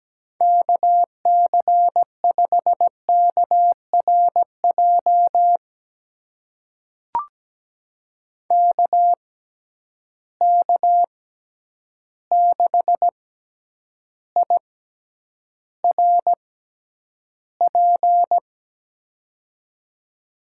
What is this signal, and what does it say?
Morse code, heard as a keyed tone, about 17 wpm, spaced as for 9 wpm, 700 Hz. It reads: KC5KRJ KK6IRP
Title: USA Call Signs - 6 Characters - 6x Character Spacing - 17wpm